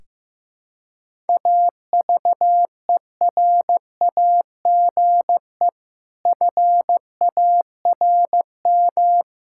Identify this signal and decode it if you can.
Morse code, heard as a keyed tone, about 15 wpm, 700 Hz. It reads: AVERAGE FARM
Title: Sentences from Top 1000 Words - Review Part 6 - 15wpm